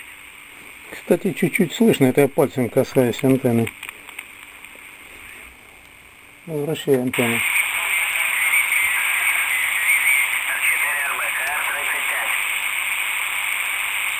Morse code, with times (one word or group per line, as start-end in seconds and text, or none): none